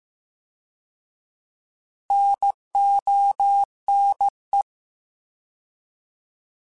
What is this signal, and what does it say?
Morse code, heard as a keyed tone, about 15 wpm, 800 Hz. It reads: NONE